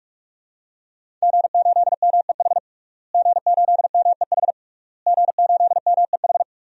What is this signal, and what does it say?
Morse code, heard as a keyed tone, about 46 wpm, 700 Hz. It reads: G8MEH G8MEH G8MEH
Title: DXCC-100 Call Signs - 5 Characters - Repeated 3x - 45wpm